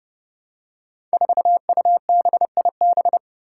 Morse code, heard as a keyed tone, about 30 wpm, 700 Hz. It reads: 4UBIB
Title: Five Challenging Letter-Number Combinations 30wpm